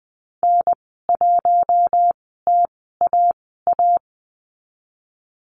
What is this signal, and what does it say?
Morse code, heard as a keyed tone, about 20 wpm, spaced as for 15 wpm, 700 Hz. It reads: N1TAA
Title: 042 Lesson 8g - Number 1 - Call Signs 20wpm